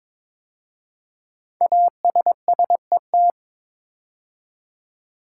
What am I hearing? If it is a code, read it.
Morse code, heard as a keyed tone, about 22 wpm, 700 Hz. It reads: ASSET
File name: 5 Letter Words - First 3 Letters Given - 22wpm